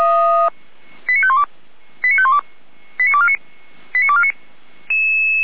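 Five-tone selective call (ZVEI), heard as quick runs of five tones, tones of about 70 ms. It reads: 89512 89512 89249 89249
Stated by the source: ZVEI five-tone selective call received on an SDR, fire brigade Feuerwehr Orlamünde.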